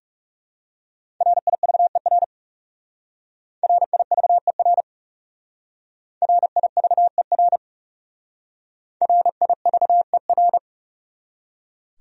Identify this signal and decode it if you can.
Morse code, heard as a keyed tone, about 37 wpm, 700 Hz. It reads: RIVER RIVER RIVER RIVER
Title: Speed-Racing Top 300 Words 30wpm